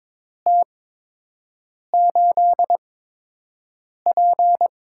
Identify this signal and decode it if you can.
Morse code, heard as a keyed tone, about 22 wpm, spaced as for 6 wpm, 700 Hz. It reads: T8P
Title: USA Call Signs - 4 Characters - 8x Character Spacing - 22wpm